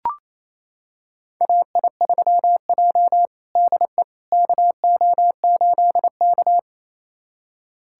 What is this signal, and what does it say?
Morse code, heard as a keyed tone, about 28 wpm, 700 Hz. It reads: AI3J DE KO8K
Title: USA Callsign DE Callsign - 4 Characters - 28wpm